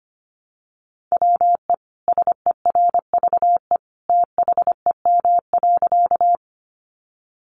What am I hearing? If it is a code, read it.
Morse code, heard as a keyed tone, about 25 wpm, 700 Hz. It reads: WE SERVE THEM.
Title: Sentences from Top 400 Words 25wpm